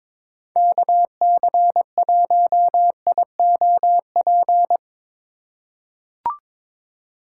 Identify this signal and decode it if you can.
Morse code, heard as a keyed tone, about 22 wpm, 700 Hz. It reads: KC1IOP
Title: USA Call Signs - 6 Characters - 2x Character Spacing - 22wpm